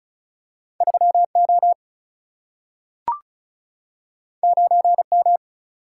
Morse code, heard as a keyed tone, about 36 wpm, 700 Hz. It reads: 3O 9M
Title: Single-Letter Single-Number 35wpm